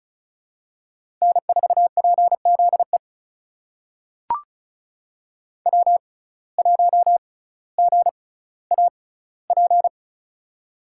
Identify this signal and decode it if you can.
Morse code, heard as a keyed tone, about 36 wpm, spaced as for 18 wpm, 700 Hz. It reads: N4PZE W1GAP
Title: USA Call Signs - 5 Characters - 6x Character Spacing - 35wpm